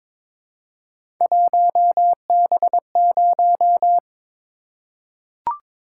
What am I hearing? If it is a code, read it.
Morse code, heard as a keyed tone, about 22 wpm, 700 Hz. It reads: 1B0
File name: practice / 22wpm / Three Letter-Number Combinations - 22wpm